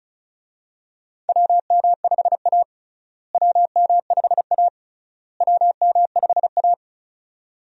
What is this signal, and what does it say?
Morse code, heard as a keyed tone, about 36 wpm, 700 Hz. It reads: WM5A WM5A WM5A
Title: USA Call Signs - 4 Characters - Repeated 3x - 35wpm